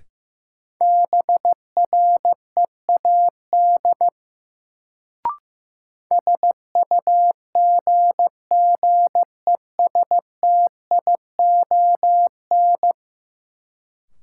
Morse code, heard as a keyed tone, about 15 wpm, 700 Hz. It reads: BREAD SUGGESTION
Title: Sentences from Top 1600 Words - New Word Drill - 15wpm